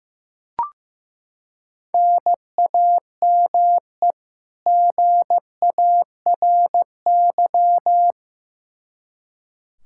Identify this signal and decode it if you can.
Morse code, heard as a keyed tone, about 15 wpm, 700 Hz. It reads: NAME GARY
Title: Practice QSO - Exchange 1 - Beginning - 15wpm